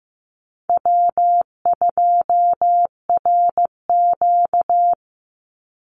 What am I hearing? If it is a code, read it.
Morse code, heard as a keyed tone, about 15 wpm, 700 Hz. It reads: W2RQ
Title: Call Signs Repeated 2x - 15wpm